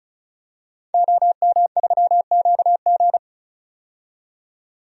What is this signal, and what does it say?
Morse code, heard as a keyed tone, about 36 wpm, 700 Hz. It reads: OM3QG
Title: DXCC-100 Call Signs - 5 Characters - 35wpm